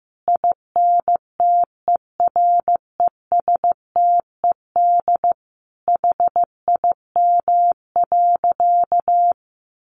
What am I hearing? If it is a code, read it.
Morse code, heard as a keyed tone, about 15 wpm, 700 Hz. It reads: INTERESTED HIM.
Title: Sentences from Top 1500 Words 15wpm